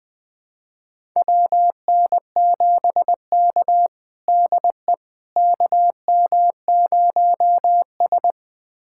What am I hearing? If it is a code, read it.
Morse code, heard as a keyed tone, about 21 wpm, 700 Hz. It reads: WN7K DE KM0S
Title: USA Callsign DE Callsign - 4 Characters - 20wpm